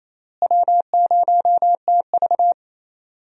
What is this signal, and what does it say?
Morse code, heard as a keyed tone, about 27 wpm, 700 Hz. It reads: W0TV
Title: USA Call Signs - 4 Characters - Repeated 3x - 28wpm